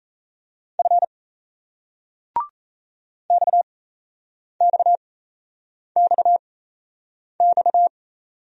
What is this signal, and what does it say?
Morse code, heard as a keyed tone, about 34 wpm, spaced as for 8 wpm, 700 Hz. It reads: F XXXX
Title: Speed-Racing Single Letters 28wpm